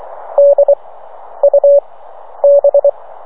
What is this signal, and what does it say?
Morse code, heard as a keyed tone, about 24 wpm, spaced as for 10 wpm, 570 Hz. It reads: DUB